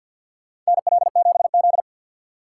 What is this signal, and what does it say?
Morse code, heard as a keyed tone, about 51 wpm, 700 Hz. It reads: NL7Z
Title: USA Call Signs - 4 Characters - 2x Character Spacing - 50wpm